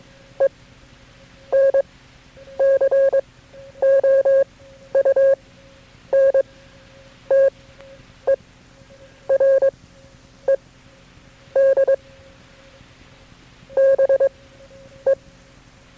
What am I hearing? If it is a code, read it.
Morse code, heard as a keyed tone, about 22 wpm, spaced as for 9 wpm, 540 Hz. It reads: ENCOUNTERED BE
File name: FreeFiles / CW-practice / KB6MT-tape3-side1